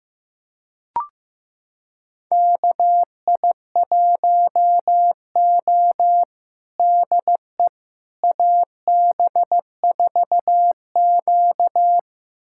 Morse code, heard as a keyed tone, about 15 wpm, 700 Hz. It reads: KI1O DE AB4Q
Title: USA Callsign DE Callsign - 4 Characters - 15wpm